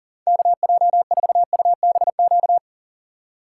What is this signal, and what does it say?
Morse code, heard as a keyed tone, about 40 wpm, 700 Hz. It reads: KJ4UBQ
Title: USA Call Signs - 6 Characters - 2x Character Spacing - 40wpm